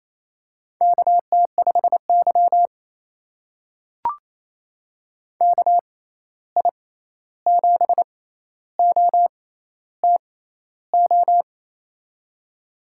Morse code, heard as a keyed tone, about 28 wpm, spaced as for 13 wpm, 700 Hz. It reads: KT5Y KI7OTO
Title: USA Call Signs - 6x Character Spacing - 28wpm